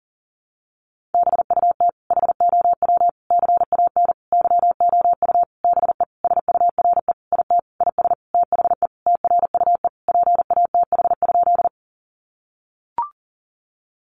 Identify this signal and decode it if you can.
Morse code, heard as a keyed tone, about 40 wpm, 700 Hz. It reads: BUT HOW CAN YOU BE SURE IT IS THE TRUE PATH?
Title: Sentences from Top 1400 Words 40wpm